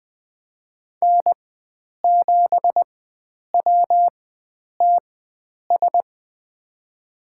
Morse code, heard as a keyed tone, about 20 wpm, spaced as for 9 wpm, 700 Hz. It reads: N7WTS
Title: USA Call Signs - 5 Characters - 4x Character Spacing - 20wpm